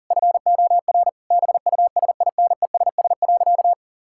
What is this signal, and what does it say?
Morse code, heard as a keyed tone, about 40 wpm, 700 Hz. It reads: FOR BUSINESS.
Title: Sentences from Top 1000 Words - Review Part 6 - 40wpm